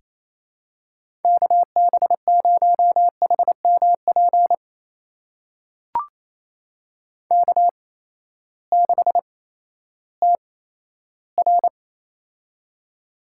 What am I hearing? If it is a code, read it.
Morse code, heard as a keyed tone, about 28 wpm, spaced as for 14 wpm, 700 Hz. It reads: KB0HMP K6TR
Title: USA Call Signs - 8x Character Spacing - 28wpm